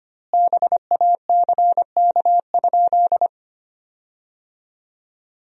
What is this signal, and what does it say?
Morse code, heard as a keyed tone, about 25 wpm, 700 Hz. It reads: BACK?
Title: 2x Word-Spacing - Sentences from Top 200 Words - 25wpm